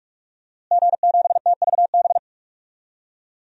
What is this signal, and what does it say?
Morse code, heard as a keyed tone, about 45 wpm, 700 Hz. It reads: G7TVB
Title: DXCC-100 Call Signs - 5 Characters - Repeated 3x - 45wpm